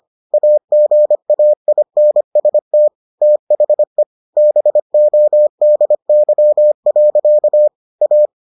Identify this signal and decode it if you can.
Morse code, heard as a keyed tone, about 25 wpm, 600 Hz. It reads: AGAINST THE BODY. A